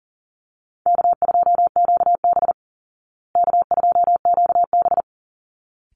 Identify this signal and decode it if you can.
Morse code, heard as a keyed tone, about 40 wpm, 700 Hz. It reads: K2QB K2QB
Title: Easy Call Signs Repeated 2x - 40wpm